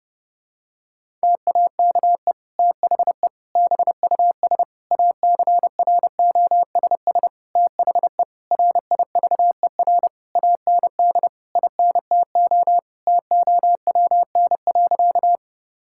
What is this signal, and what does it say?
Morse code, heard as a keyed tone, about 30 wpm, 700 Hz. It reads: TAKE THE BUS ACROSS THE RIVER AND INTO TOWN.
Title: Sentences from Top 1000 Words - Review Part 6 - 30wpm